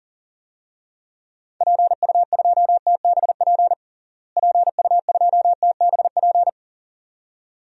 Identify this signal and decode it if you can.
Morse code, heard as a keyed tone, about 37 wpm, 700 Hz. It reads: PU2TBP PU2TBP
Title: DXCC-100 Call Signs - 6 Characters - Repeated 2x - 40wpm